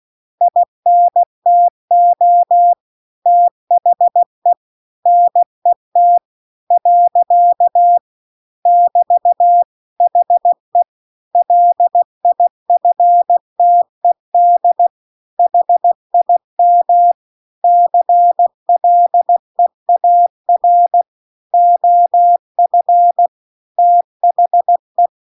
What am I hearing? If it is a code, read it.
Morse code, heard as a keyed tone, about 16 wpm, 700 Hz. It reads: INTO THE NET . <BT> HE LIFTED HIM CLEAR OF THE